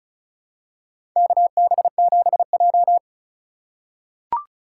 Morse code, heard as a keyed tone, about 34 wpm, 700 Hz. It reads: KB7J
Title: USA Call Signs - 4 Characters - 4x Character Spacing - 35wpm